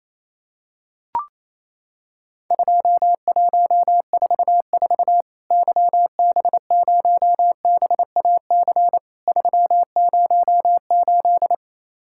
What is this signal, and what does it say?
Morse code, heard as a keyed tone, about 28 wpm, 700 Hz. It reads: 2144 YB0BAC 308
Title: Worked All Europe DX Contest - QTC Exchange - 28wpm